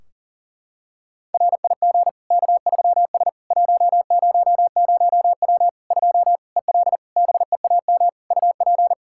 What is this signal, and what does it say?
Morse code, heard as a keyed tone, about 40 wpm, 700 Hz. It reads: RIG K3S 100W 2 EL BEAM UP
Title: Practice QSO - Exchange 2 - Info - 40wpm